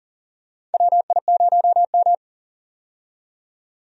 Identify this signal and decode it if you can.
Morse code, heard as a keyed tone, about 40 wpm, 700 Hz. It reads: WI0M
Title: USA Call Signs - 4 Characters - 40wpm